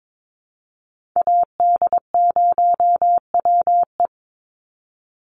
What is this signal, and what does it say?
Morse code, heard as a keyed tone, about 22 wpm, 700 Hz. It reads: AD0WE
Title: Call Signs Repeated 2x - 22wpm